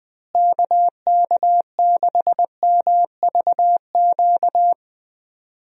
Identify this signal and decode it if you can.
Morse code, heard as a keyed tone, about 21 wpm, 700 Hz. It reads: KK6MVQ
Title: USA Call Signs - 6 Characters - Repeated 3x - 20wpm